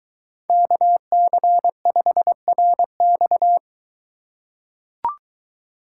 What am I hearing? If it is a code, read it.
Morse code, heard as a keyed tone, about 23 wpm, 700 Hz. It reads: KC5RX